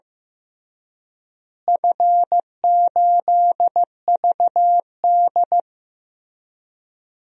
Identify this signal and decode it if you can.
Morse code, heard as a keyed tone, about 15 wpm, 700 Hz. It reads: F8VD
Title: DXCC-100 Call Signs - 4 Characters - Repeated 3x - 15wpm